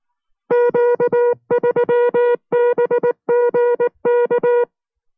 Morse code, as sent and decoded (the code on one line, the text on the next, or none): --.- ...-- -... --. -.-
Q3BGK